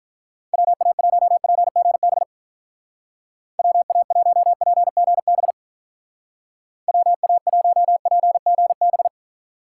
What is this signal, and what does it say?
Morse code, heard as a keyed tone, about 44 wpm, 700 Hz. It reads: WA1PGB WA1PGB WA1PGB